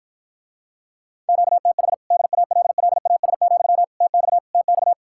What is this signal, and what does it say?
Morse code, heard as a keyed tone, about 53 wpm, 700 Hz. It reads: QTH DALLAS, TX TX